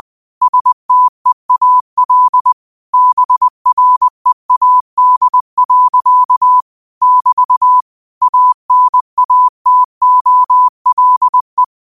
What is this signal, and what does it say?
Morse code, heard as a keyed tone, about 20 wpm, 1000 Hz. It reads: STEAL BREAD. <BT> ANATOLE